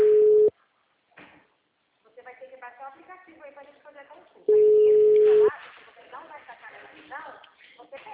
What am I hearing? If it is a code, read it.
Ringback tone (Europe, 425 Hz), with speech.
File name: out- -2009- - - .94851